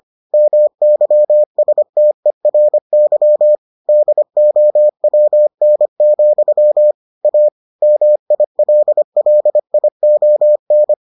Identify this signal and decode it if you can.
Morse code, heard as a keyed tone, about 25 wpm, 600 Hz. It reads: MYSTERY DOWN, A MILLION